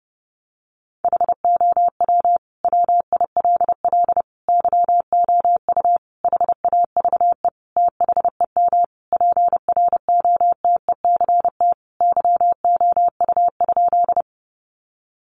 Morse code, heard as a keyed tone, about 30 wpm, 700 Hz. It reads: HOW WILL YOU HAVE THEM PROTECT YOU?
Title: Sentences from Top 900 Words 30wpm